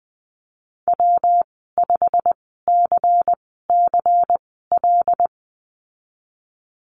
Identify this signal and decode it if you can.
Morse code, heard as a keyed tone, about 20 wpm, spaced as for 15 wpm, 700 Hz. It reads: W5CCL
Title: 128 Lesson 21g - Number 6 - Call Signs 20wpm